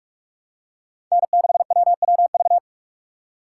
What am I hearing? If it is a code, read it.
Morse code, heard as a keyed tone, about 46 wpm, 700 Hz. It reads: N6WWV
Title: USA Call Signs - 5 Characters - 6x Character Spacing - 45wpm